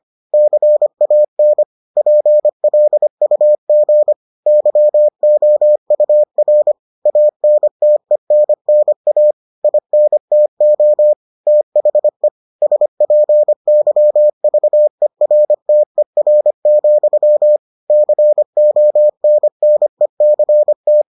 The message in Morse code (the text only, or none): CAN PLUG YOUR ANTENNA INTO THE SPYVERTER, CONNECT